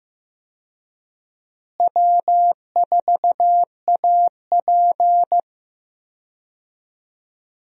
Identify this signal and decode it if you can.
Morse code, heard as a keyed tone, about 15 wpm, 700 Hz. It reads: W4AP
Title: USA Call Signs - 4 Characters - Repeated 3x - 15wpm